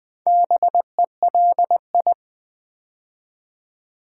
Morse code, heard as a keyed tone, about 20 wpm, 690 Hz. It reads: BELI
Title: Build A Word - 2 Letters At A Time - 6-Letter Words - 20wpm